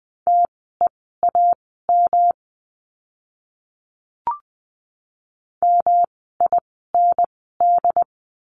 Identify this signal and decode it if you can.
Morse code, heard as a keyed tone, about 20 wpm, spaced as for 14 wpm, 700 Hz. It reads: TEAM MIND
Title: 108 Lesson 18f - Letter m - Words 20wpm